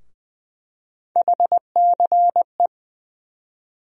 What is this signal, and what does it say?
Morse code, heard as a keyed tone, about 20 wpm, 700 Hz. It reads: HCE
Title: USA Call Sign Postfixes - 3 Letters 20wpm